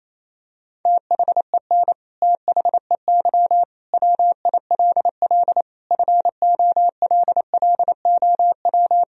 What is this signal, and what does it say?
Morse code, heard as a keyed tone, about 28 wpm, 700 Hz. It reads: THEN THEY WILL FOLLOW
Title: Sentences from Top 200 Words and Abbreviations 28wpm